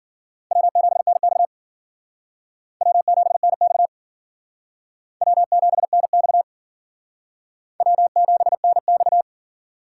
Morse code, heard as a keyed tone, about 50 wpm, 700 Hz. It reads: W7NX W7NX W7NX W7NX
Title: Speed Racing - USA 1x2 Call Signs 40wpm